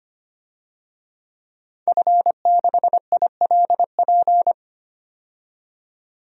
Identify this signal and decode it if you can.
Morse code, heard as a keyed tone, about 25 wpm, 700 Hz. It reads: F6ILP
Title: DXCC-100 Call Signs - 5 Characters - 25wpm